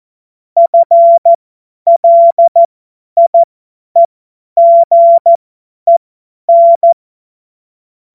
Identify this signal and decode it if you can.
Morse code, heard as a keyed tone, about 14 wpm, spaced as for 10 wpm, 680 Hz. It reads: FLIEGEN
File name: Morsekurs / Witze / Witz 06. Januar 2026 50 F70